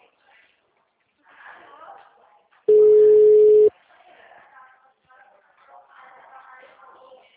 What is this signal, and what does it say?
Ringback tone (Europe, 425 Hz), with speech.